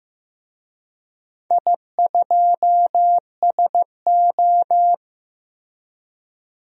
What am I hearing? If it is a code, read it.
Morse code, heard as a keyed tone, about 15 wpm, 700 Hz. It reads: I2SO
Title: DXCC-100 Call Signs - 4 Characters - Repeated 3x - 15wpm